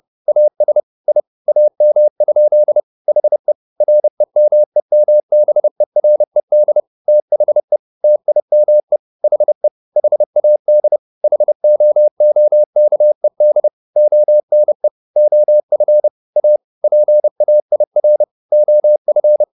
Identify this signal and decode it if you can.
Morse code, heard as a keyed tone, about 30 wpm, 600 Hz. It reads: AS I AM? HE REMEMBERED THE TIME HE HAD HOOKED ONE OF A PAIR OF
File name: the old man and the sea / the old man and the sea 1 30wpm